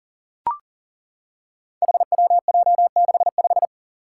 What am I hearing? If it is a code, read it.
Morse code, heard as a keyed tone, about 40 wpm, 700 Hz. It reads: HWJ65
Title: Five Challenging Letter-Number Combinations 40wpm